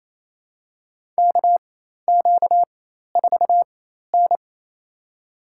Morse code, heard as a keyed tone, about 28 wpm, spaced as for 13 wpm, 700 Hz. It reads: KQ4N
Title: USA Call Signs - 4 Characters - 4x Character Spacing - 28wpm